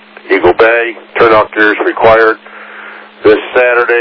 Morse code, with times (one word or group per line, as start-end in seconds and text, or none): none